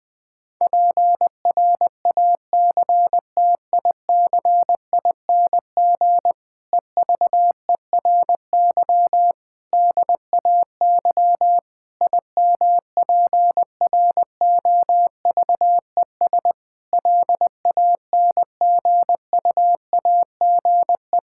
Morse code, heard as a keyed tone, about 20 wpm, 700 Hz. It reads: PRACTICING EVERY DAY IMPROVES LANGUAGE
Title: Difficult Sentences with Hints - 20wpm